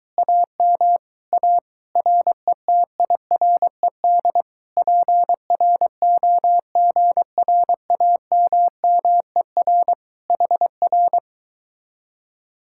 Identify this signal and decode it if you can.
Morse code, heard as a keyed tone, about 24 wpm, 700 Hz. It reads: AM A RETIRED PROGRAMMER HR